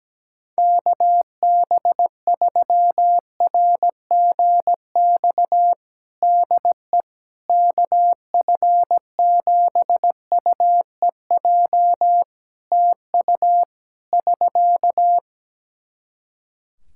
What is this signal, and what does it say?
Morse code, heard as a keyed tone, about 17 wpm, 700 Hz. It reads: KB3RGX DE KF7UEJ TU <SK>
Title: Practice QSO - Exchange 4 - Ending - 17wpm